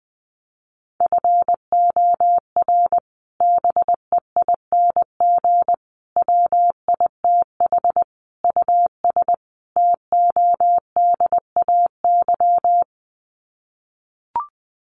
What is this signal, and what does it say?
Morse code, heard as a keyed tone, about 20 wpm, 700 Hz. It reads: FOR BEING WITH US TODAY
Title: Sets of 5 Words Encore2 20wpm